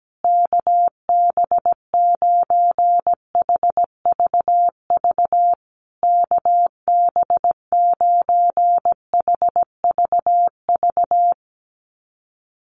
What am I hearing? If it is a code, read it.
Morse code, heard as a keyed tone, about 17 wpm, 700 Hz. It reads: KB9HVV KB9HVV
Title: USA Call Signs Repeated 3x 17wpm